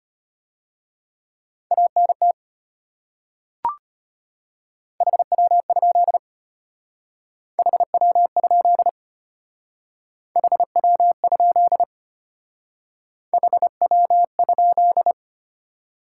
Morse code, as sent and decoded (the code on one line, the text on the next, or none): .- -. - / .... .-- ..--.. / .... .-- ..--.. / .... .-- ..--.. / .... .-- ..--..
ANT HW? HW? HW? HW?